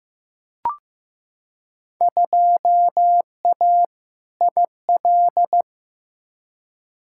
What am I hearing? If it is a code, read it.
Morse code, heard as a keyed tone, about 16 wpm, 700 Hz. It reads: 2A IL
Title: Field Day Exchanges - 15wpm